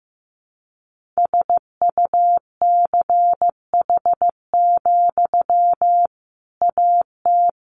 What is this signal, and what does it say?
Morse code, heard as a keyed tone, about 15 wpm, 700 Hz. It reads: SUCH, AT
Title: Sentences from Top 1500 Words 15wpm